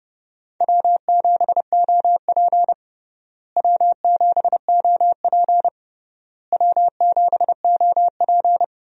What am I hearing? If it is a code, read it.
Morse code, heard as a keyed tone, about 32 wpm, 700 Hz. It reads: W7OP W7OP W7OP
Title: USA Call Signs - 4 Characters - Repeated 3x - 30wpm